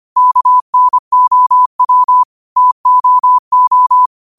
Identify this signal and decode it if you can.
Morse code, heard as a keyed tone, about 25 wpm, 1000 Hz. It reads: KNOW TOO